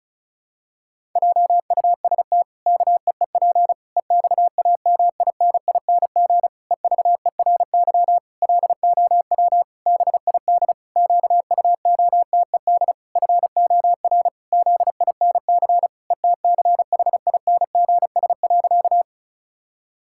Morse code, heard as a keyed tone, about 34 wpm, 700 Hz. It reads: JUST KEEP EXAMINING EVERY LOW BID QUOTED FOR ZINC ETCHINGS.
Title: Pangram Challenge v2 - 35wpm